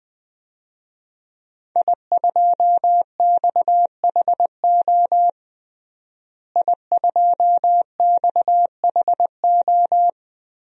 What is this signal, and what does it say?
Morse code, heard as a keyed tone, about 21 wpm, 700 Hz. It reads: I2XHO I2XHO
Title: DXCC-100 Call Signs - 5 Characters - Repeated 3x - 20wpm